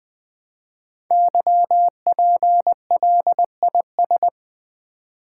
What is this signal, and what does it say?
Morse code, heard as a keyed tone, about 20 wpm, 700 Hz. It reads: YPLIS
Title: Code Groups - 5 Letters - 20wpm